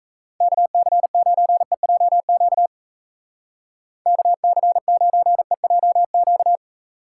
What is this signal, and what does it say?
Morse code, heard as a keyed tone, about 37 wpm, 700 Hz. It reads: KC9EJQ KC9EJQ